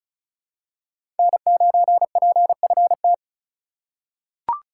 Morse code, heard as a keyed tone, about 34 wpm, 700 Hz. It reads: N9PFT
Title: USA Call Signs - 5 Characters - 4x Character Spacing - 35wpm